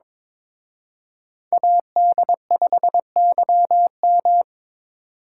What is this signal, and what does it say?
Morse code, heard as a keyed tone, about 26 wpm, 690 Hz. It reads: AD5YM
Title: Speed Racing - USA Call Signs 22wpm